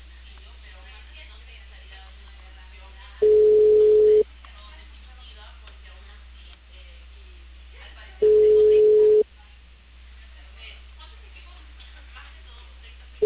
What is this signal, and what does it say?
Ringback tone (Europe, 425 Hz), with speech.